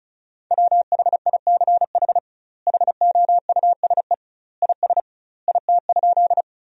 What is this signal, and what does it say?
Morse code, heard as a keyed tone, about 36 wpm, 700 Hz. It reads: WHICH HOUSE IS IT?
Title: 2x Word-Spacing - Sentences from Top 200 Words - 35wpm